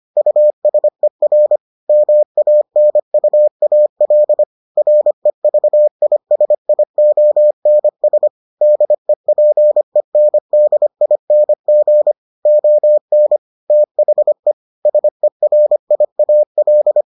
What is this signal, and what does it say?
Morse code, heard as a keyed tone, about 25 wpm, 600 Hz. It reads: USER MANUAL REVISIONS DEPENDING ON THE SERIAL